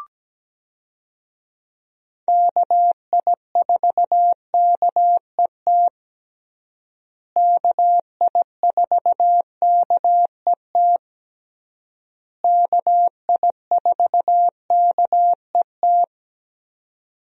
Morse code, heard as a keyed tone, about 17 wpm, 700 Hz. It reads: KI4KET KI4KET KI4KET
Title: USA Call Signs - 6 Characters - Repeated 3x - 17wpm